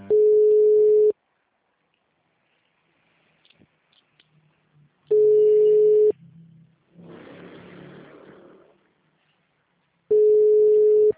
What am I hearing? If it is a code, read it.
Ringback tone (Europe, 425 Hz).
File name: out- -2040- - - .13485